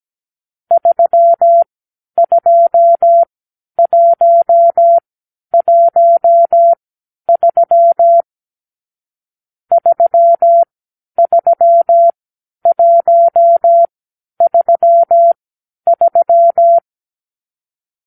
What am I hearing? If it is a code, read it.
Morse code, heard as a keyed tone, about 17 wpm, spaced as for 10 wpm, 680 Hz. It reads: 32113 33133
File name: k7qo / Lesson 44 (1-3)